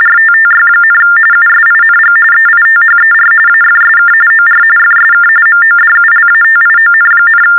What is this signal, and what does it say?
TTY/TDD text telephone tones, about 45.45 baud. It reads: RTTY, USING EITHER AFSK OR FSK MODULATION,
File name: teletype-RTTY-45